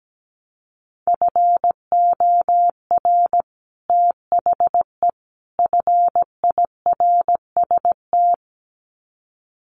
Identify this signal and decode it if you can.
Morse code, heard as a keyed tone, about 17 wpm, 700 Hz. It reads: FOR THE FIRST
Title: Sets of 3 Words 17wpm